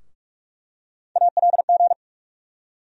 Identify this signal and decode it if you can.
Morse code, heard as a keyed tone, about 46 wpm, 700 Hz. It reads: ALG
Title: USA Call Sign Postfixes - 3 Letters 45wpm